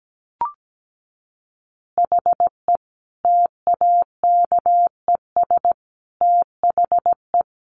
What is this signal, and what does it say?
Morse code, heard as a keyed tone, about 17 wpm, 700 Hz. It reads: HE TAKES THE
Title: Additional Easy Phrases 17wpm